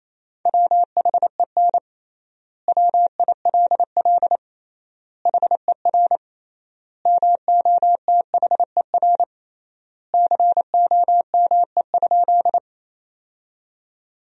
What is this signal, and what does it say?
Morse code, heard as a keyed tone, about 28 wpm, 700 Hz. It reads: WHEN WILL HER MOTHER COME?
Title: 3x Word-Spacing - Sentences from Top 200 Words - 28wpm